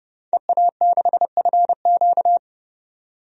Morse code, heard as a keyed tone, about 30 wpm, 700 Hz. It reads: EA6FQ
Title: DXCC-100 Call Signs - 5 Characters - 30wpm